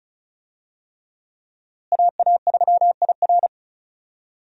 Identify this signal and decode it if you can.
Morse code, heard as a keyed tone, about 36 wpm, 700 Hz. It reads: AA3IR
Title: USA Call Signs - 5 Characters - Repeated 3x - 35wpm